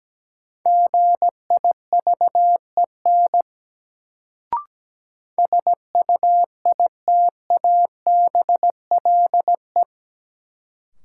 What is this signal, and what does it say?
Morse code, heard as a keyed tone, about 17 wpm, 700 Hz. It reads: GIVEN SUITABLE
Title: Sentences from Top 1400 Words - New Word Drill - 17wpm